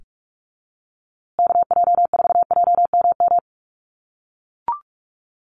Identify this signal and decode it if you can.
Morse code, heard as a keyed tone, about 44 wpm, 700 Hz. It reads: KJ4JMM
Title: USA Call Signs Repeated 3x 45wpm